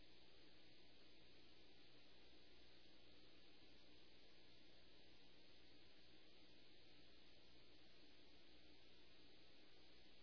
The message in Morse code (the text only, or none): none